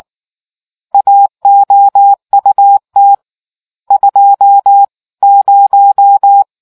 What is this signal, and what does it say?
Morse code, heard as a keyed tone, about 19 wpm, 800 Hz. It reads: AOUT 20